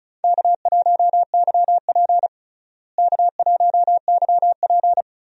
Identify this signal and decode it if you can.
Morse code, heard as a keyed tone, about 34 wpm, 700 Hz. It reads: K1YP K1YP
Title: USA Call Signs - 4 Characters - Repeated 3x - 35wpm